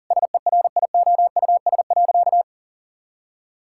Morse code, heard as a keyed tone, about 40 wpm, 700 Hz. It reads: SERIOUS.
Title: Sentences from Top 1600 Words 40wpm